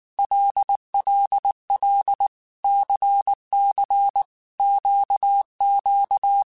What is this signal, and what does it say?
Morse code, heard as a keyed tone, about 19 wpm, 800 Hz. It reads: LLL CC QQ
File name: Groups@18 / Groups@18 mp3 / LFCYQ@18